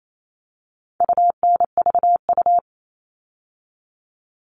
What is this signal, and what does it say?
Morse code, heard as a keyed tone, about 28 wpm, 700 Hz. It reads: UNVU
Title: All Cut Numbers - 4-Digit 28wpm